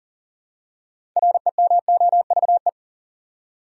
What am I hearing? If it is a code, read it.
Morse code, heard as a keyed tone, about 40 wpm, 700 Hz. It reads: REMOVE
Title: Six Letter Words - 40wpm